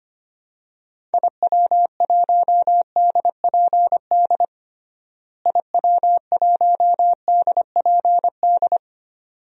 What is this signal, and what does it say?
Morse code, heard as a keyed tone, about 25 wpm, 700 Hz. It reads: IW1DPD IW1DPD
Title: DXCC-100 Call Signs - 6 Characters - Repeated 2x - 25wpm